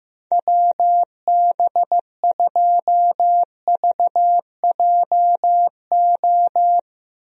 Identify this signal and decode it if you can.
Morse code, heard as a keyed tone, about 15 wpm, 700 Hz. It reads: WB2VJO
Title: USA Call Signs - 6 Characters - Repeated 2x - 15wpm